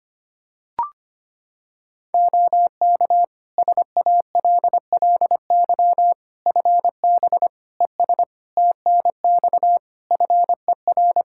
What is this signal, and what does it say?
Morse code, heard as a keyed tone, about 25 wpm, 700 Hz. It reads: OK SALLY FB ES TNX FER
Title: Practice QSO - Exchange 2 - Info - 25wpm